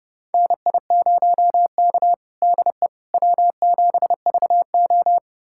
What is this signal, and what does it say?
Morse code, heard as a keyed tone, about 31 wpm, 700 Hz. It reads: NI0K DE W7VO
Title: USA Callsign DE Callsign - 4 Characters - 30wpm